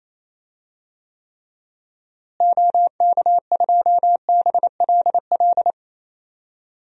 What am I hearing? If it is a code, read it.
Morse code, heard as a keyed tone, about 27 wpm, 700 Hz. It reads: OK2BLL
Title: DXCC-100 Call Signs - 6 Characters - Repeated 3x - 28wpm